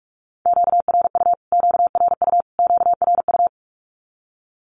Morse code, heard as a keyed tone, about 44 wpm, 700 Hz. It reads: QRU QRU QRU
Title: Q-Codes Repeated 3x 45wpm